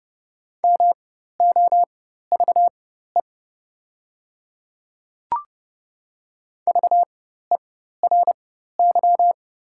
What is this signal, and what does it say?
Morse code, heard as a keyed tone, about 30 wpm, spaced as for 14 wpm, 690 Hz. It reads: MOVE VERY
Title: 4x Character Spacing - Top 200 Words - 30wpm